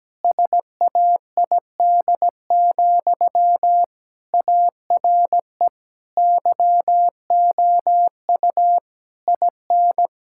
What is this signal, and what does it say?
Morse code, heard as a keyed tone, about 17 wpm, 700 Hz. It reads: SAID, ARE YOU IN
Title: Sentences from Top 1600 Words 17wpm